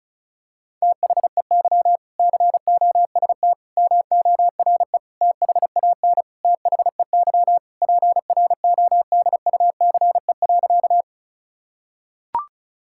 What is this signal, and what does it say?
Morse code, heard as a keyed tone, about 34 wpm, 700 Hz. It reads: THEY COST MORE THAN THEY PRODUCE.
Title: Sentences from Top 1000 Words - Review Part 5 - 35wpm